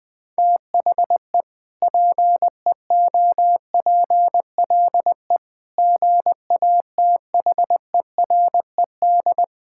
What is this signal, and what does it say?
Morse code, heard as a keyed tone, about 21 wpm, 700 Hz. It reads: THE PEOPLE GATHERED
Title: Sentences with 2-Word Phrases - Spoken Beginning - 20wpm